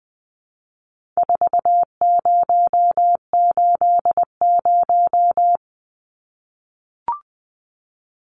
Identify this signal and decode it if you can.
Morse code, heard as a keyed tone, about 20 wpm, 700 Hz. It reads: 4080
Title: Numbers - 4-Digit 20wpm